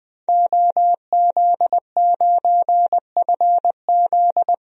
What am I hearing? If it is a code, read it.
Morse code, heard as a keyed tone, about 20 wpm, 700 Hz. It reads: OZ9FZ
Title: DXCC-100 Call Signs - 5 Characters - Repeated 3x - 20wpm